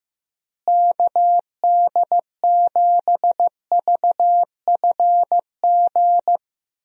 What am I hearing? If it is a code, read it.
Morse code, heard as a keyed tone, about 15 wpm, 700 Hz. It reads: KD7VFG
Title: USA Call Signs - 6 Characters - Repeated 2x - 15wpm